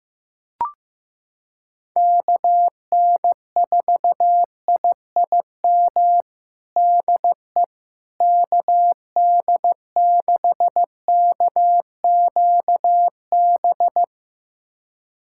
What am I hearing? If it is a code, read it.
Morse code, heard as a keyed tone, about 15 wpm, 700 Hz. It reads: KN4IIM DE KD6KQB
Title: USA Callsign DE Callsign 15wpm